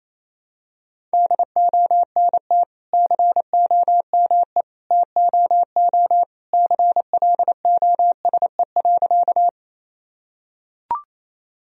Morse code, heard as a keyed tone, about 28 wpm, 700 Hz. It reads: DONT COME TOO CLOSE.
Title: Sentences from Top 1000 Words - Review Part 1 - 28wpm